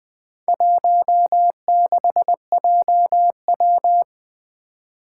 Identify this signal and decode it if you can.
Morse code, heard as a keyed tone, about 20 wpm, 700 Hz. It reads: 16JW
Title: Four Challenging Letter-Number Combinations 20wpm